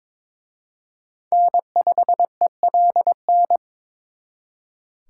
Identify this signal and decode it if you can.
Morse code, heard as a keyed tone, about 26 wpm, 700 Hz. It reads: N5ELN
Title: Speed Racing - USA 1x3 Call Signs 22wpm